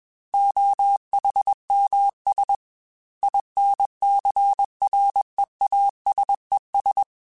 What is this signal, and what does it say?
Morse code, heard as a keyed tone, about 21 wpm, 800 Hz. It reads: OHMS INCREASES
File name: Technical Subjects / Technical Subjects-mp3s / XL @20